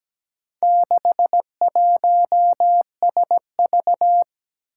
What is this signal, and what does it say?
Morse code, heard as a keyed tone, about 17 wpm, 700 Hz. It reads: 61SV
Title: Four Challenging Letter-Number Combinations 17wpm